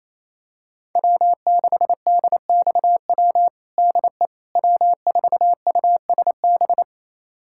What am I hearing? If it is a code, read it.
Morse code, heard as a keyed tone, about 28 wpm, 700 Hz. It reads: W6DXW DE W4USB
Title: USA Callsign DE Callsign - 5 Characters - 28wpm